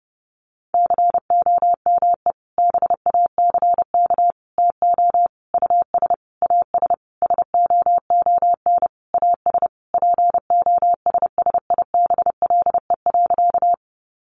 Morse code, heard as a keyed tone, about 30 wpm, 700 Hz. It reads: COME BACK TO US AS SOON AS POSSIBLE.
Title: Sentences from Top 500 Words 30wpm